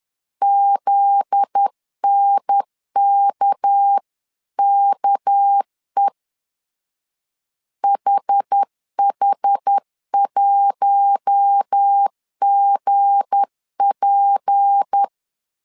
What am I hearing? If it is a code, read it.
Morse code, heard as a keyed tone, about 10 wpm, 800 Hz. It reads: ZNKKE HH1GP
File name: CW / DET / 1 9 1500C S A 10wpm 4511